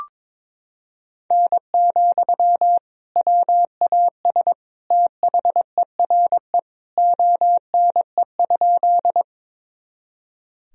Morse code, heard as a keyed tone, about 22 wpm, 700 Hz. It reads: N, WAS THERE ONE?
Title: Sentences from Top 100 Words and Abbreviations 22wpm